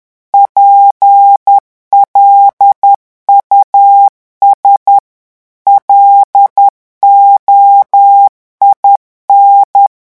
Morse code, heard as a keyed tone, about 10 wpm, 800 Hz. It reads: PLUS LOIN
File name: morse mp3 / revision francais 10 mots minute